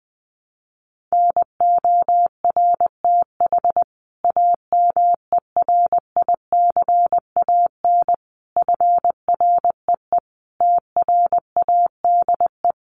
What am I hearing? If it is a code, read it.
Morse code, heard as a keyed tone, about 19 wpm, 700 Hz. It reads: NORTH AMERICAN FREE TRADE
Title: Sets of 5 Words Encore 20wpm